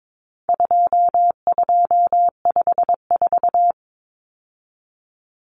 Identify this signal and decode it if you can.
Morse code, heard as a keyed tone, about 22 wpm, 700 Hz. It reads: 2254
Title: Common Cut Numbers - 4-Digit 22wpm